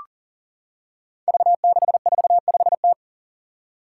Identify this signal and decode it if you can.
Morse code, heard as a keyed tone, about 44 wpm, 700 Hz. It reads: V645T
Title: Five Challenging Letter-Number Combinations 40wpm